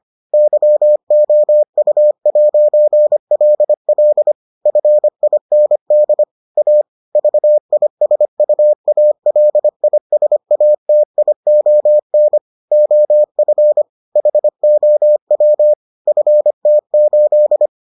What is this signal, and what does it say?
Morse code, heard as a keyed tone, about 25 wpm, 600 Hz. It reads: YOU'LL FIND A VISUALISATION OF HOW FT8